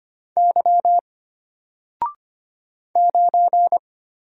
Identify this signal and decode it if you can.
Morse code, heard as a keyed tone, about 25 wpm, 700 Hz. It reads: Y 9
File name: Single-Letter-Number-25wpm